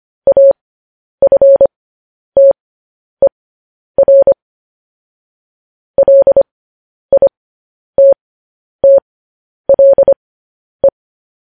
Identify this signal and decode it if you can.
Morse code, heard as a keyed tone, about 25 wpm, spaced as for 10 wpm, 550 Hz. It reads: AFTER LITTLE
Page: News Headlines in Morse Code at 10 WPM